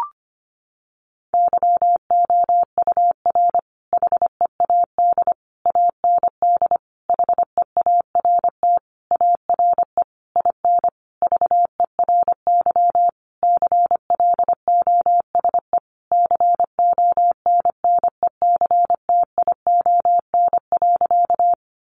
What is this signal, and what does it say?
Morse code, heard as a keyed tone, about 25 wpm, 700 Hz. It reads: YOUR HEAD AND HEART ARE IN VERY CLOSE CONNECTION.
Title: Sentences from Top 1300 Words 25wpm